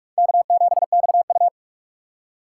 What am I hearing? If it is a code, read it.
Morse code, heard as a keyed tone, about 44 wpm, 700 Hz. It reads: K7XU
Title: USA Call Signs - 4 Characters - 4x Character Spacing - 45wpm